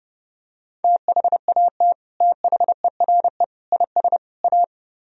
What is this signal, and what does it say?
Morse code, heard as a keyed tone, about 30 wpm, 700 Hz. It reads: THAT THERE IS A